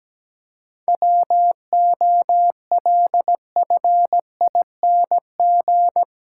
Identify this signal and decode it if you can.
Morse code, heard as a keyed tone, about 17 wpm, 700 Hz. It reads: WOLFING
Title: Pangram Challenge v2 - 17wpm